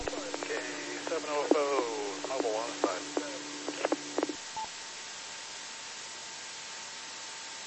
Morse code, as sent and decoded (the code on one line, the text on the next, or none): none